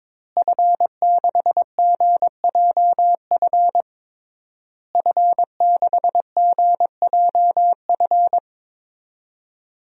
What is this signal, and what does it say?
Morse code, heard as a keyed tone, about 22 wpm, 700 Hz. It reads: F6GJF F6GJF
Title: DXCC-100 Call Signs - 5 Characters - Repeated 3x - 22wpm